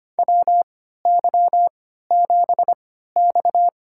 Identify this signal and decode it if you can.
Morse code, heard as a keyed tone, about 25 wpm, spaced as for 14 wpm, 700 Hz. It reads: WY7X
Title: USA Call Signs - 4 Characters - 3x Character Spacing - 25wpm